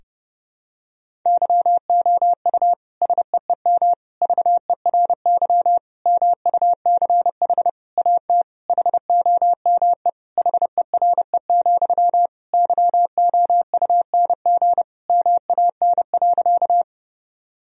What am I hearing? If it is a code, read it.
Morse code, heard as a keyed tone, about 30 wpm, 700 Hz. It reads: YOU SEEM VERY MUCH AT HOME HERE, YOUNG MAN.
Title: Sentences from Top 1000 Words - Review Part 1 - 30wpm